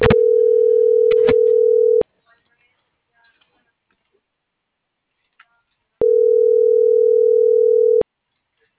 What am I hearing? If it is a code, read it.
Ringback tone (North America), 440 + 480 Hz.